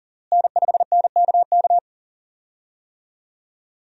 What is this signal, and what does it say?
Morse code, heard as a keyed tone, about 40 wpm, 700 Hz. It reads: N5NKK
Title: USA Call Signs - 5 Characters - Repeated 2x - 40wpm